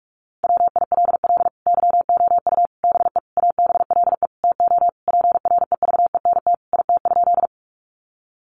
Morse code, heard as a keyed tone, about 44 wpm, 700 Hz. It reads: WILL YOU BE ABLE TO PREVENT IT?
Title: Sentences from Top 1100 Words 45wpm